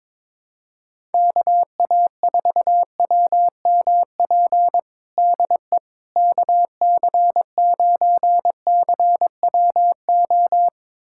Morse code, heard as a keyed tone, about 22 wpm, 700 Hz. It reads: KA4WMP DE KC9CWO
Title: USA Callsign DE Callsign - 6 Characters - 22wpm